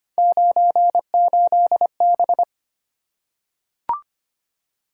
Morse code, heard as a keyed tone, about 25 wpm, 700 Hz. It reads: 98B
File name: Three Letter-Number Combinations - 25wpm